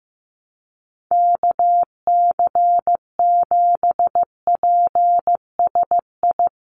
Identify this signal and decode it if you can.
Morse code, heard as a keyed tone, about 15 wpm, 700 Hz. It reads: KC7PSI
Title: USA Call Signs Repeated 3x 15wpm